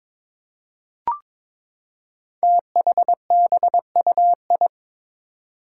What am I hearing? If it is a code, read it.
Morse code, heard as a keyed tone, about 22 wpm, 700 Hz. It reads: THBUI
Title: Five Challenging Letter-Number Combinations 22wpm